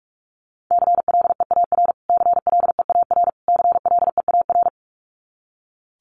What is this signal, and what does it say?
Morse code, heard as a keyed tone, about 44 wpm, 700 Hz. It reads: CLEAR CLEAR CLEAR
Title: Top 500 Words Repeated 45wpm